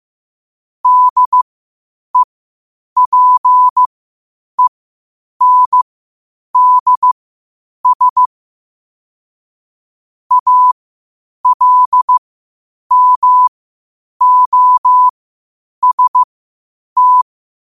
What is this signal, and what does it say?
Morse code, heard as a keyed tone, about 15 wpm, spaced as for 8 wpm, 1000 Hz. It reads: DEPENDS ALMOST
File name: qotdfiles / 8wpm